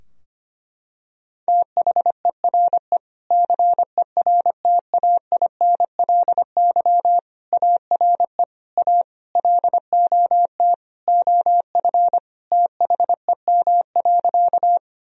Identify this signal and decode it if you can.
Morse code, heard as a keyed tone, about 25 wpm, 700 Hz. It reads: THERE CERTAINLY ARE A LOT OF THEM.
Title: Sentences from Top 1000 Words - Review Part 3 - 25wpm